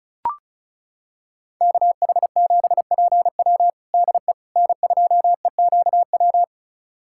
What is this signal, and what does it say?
Morse code, heard as a keyed tone, about 36 wpm, 700 Hz. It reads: KH7PW DE N2EQW
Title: USA Callsign DE Callsign - 5 Characters - 35wpm